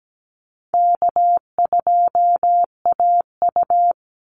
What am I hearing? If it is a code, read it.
Morse code, heard as a keyed tone, about 17 wpm, 700 Hz. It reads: K2AU
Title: Easy Call Signs - 17wpm